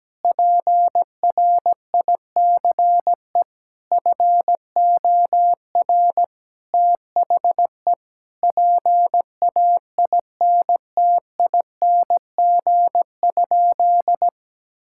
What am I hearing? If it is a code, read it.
Morse code, heard as a keyed tone, about 17 wpm, 700 Hz. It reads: PRICE FOR THE PAINTING?
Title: Sentences from Top 1000 Words - Review Part 6 - 17wpm